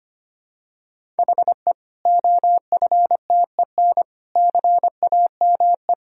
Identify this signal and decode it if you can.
Morse code, heard as a keyed tone, about 25 wpm, 700 Hz. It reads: HE OFTEN CAME
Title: Sentences from Top 1000 Words - Review Part 1 - 25wpm